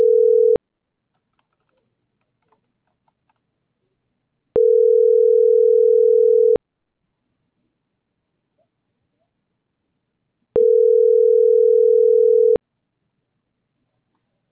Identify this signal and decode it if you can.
Ringback tone (North America), 440 + 480 Hz.